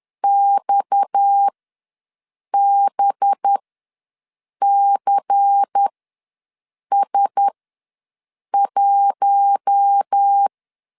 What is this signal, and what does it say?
Morse code, heard as a keyed tone, about 10 wpm, spaced as for 6 wpm, 800 Hz. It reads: XBCS1
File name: LEZIONI / 8 YJ45 / 8 Y ws 10wpm 3952